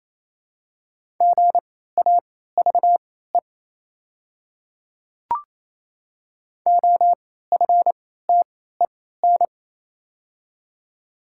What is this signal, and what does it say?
Morse code, heard as a keyed tone, about 28 wpm, spaced as for 16 wpm, 700 Hz. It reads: GAVE OFTEN
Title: 3x Character Spacing - Top 500 Words - 28wpm